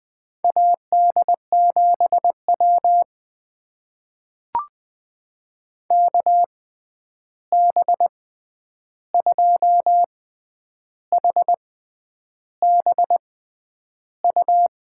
Morse code, heard as a keyed tone, about 20 wpm, spaced as for 9 wpm, 700 Hz. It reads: AD7W KB2HBU
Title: USA Call Signs - 6x Character Spacing - 20wpm